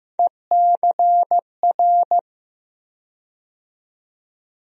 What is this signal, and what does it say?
Morse code, heard as a keyed tone, about 15 wpm, 700 Hz. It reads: ECR
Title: USA Call Sign Postfixes - 3 Letters 15wpm